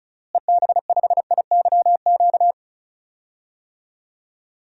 Morse code, heard as a keyed tone, about 36 wpm, 700 Hz. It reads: EB5IYQ
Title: DXCC-100 Call Signs - 6 Characters - Repeated 2x - 35wpm